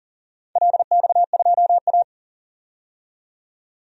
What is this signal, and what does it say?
Morse code, heard as a keyed tone, about 40 wpm, 700 Hz. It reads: LX2A
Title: DXCC-100 Call Signs - 4 Characters - 40wpm